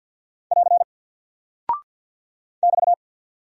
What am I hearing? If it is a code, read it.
Morse code, heard as a keyed tone, about 48 wpm, 700 Hz. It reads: <AR> <BT>
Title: Single Character (Letter:Number:Punctuation:Prosign) 50wpm